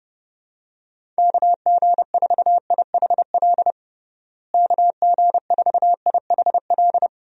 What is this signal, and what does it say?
Morse code, heard as a keyed tone, about 30 wpm, 700 Hz. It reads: KG4IHL KG4IHL
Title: USA Call Signs - 6 Characters - Repeated 2x - 30wpm